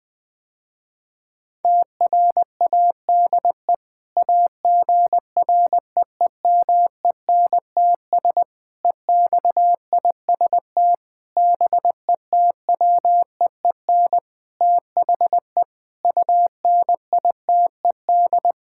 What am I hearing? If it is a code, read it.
Morse code, heard as a keyed tone, about 20 wpm, 700 Hz. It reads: TRADE AGREEMENTS EXIST BETWEEN THE UNITED
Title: Sentences with 5-Word Phrases - Spoken Beginning - 20wpm